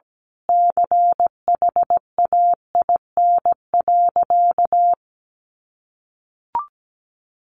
Morse code, heard as a keyed tone, about 17 wpm, 700 Hz. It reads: CHAIN.
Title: Sentences from Top 1500 Words 17wpm